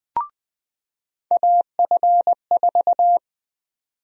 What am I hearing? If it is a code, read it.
Morse code, heard as a keyed tone, about 20 wpm, 700 Hz. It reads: AF4
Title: USA Call Sign Prefixes - 20wpm